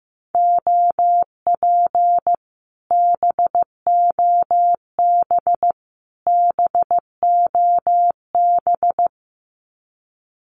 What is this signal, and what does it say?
Morse code, heard as a keyed tone, about 15 wpm, 700 Hz. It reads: OP BOB BOB
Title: Common QSO Phrases Encore 15wpm